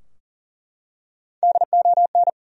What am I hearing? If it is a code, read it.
Morse code, heard as a keyed tone, about 40 wpm, 700 Hz. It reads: DON
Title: USA Common Names - Max 3 Letters - 40wpm